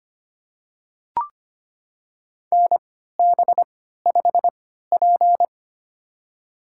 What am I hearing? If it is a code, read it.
Morse code, heard as a keyed tone, about 25 wpm, spaced as for 14 wpm, 700 Hz. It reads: NB5P
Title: USA Call Signs - 3x Character Spacing - 25wpm